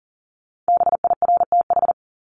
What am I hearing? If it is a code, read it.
Morse code, heard as a keyed tone, about 40 wpm, 700 Hz. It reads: BIRTH
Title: Top 751-1000 Words 40wpm